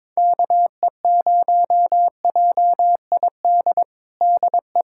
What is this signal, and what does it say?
Morse code, heard as a keyed tone, about 22 wpm, 700 Hz. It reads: KE0JID DE
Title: USA Callsign DE Callsign - 6 Characters - 22wpm